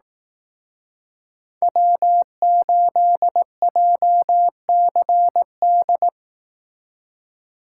Morse code, heard as a keyed tone, about 18 wpm, 700 Hz. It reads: W8JCD
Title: Speed Racing - USA 1x3 Call Signs 15wpm